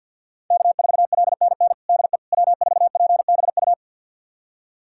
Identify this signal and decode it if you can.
Morse code, heard as a keyed tone, about 50 wpm, 700 Hz. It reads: K4LNN DE W4PBU
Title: USA Callsign DE Callsign - 5 Characters - 50wpm